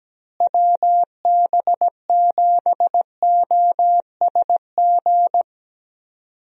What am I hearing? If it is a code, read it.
Morse code, heard as a keyed tone, about 17 wpm, 700 Hz. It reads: WB7OSG
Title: USA Call Signs - 6 Characters - Repeated 3x - 17wpm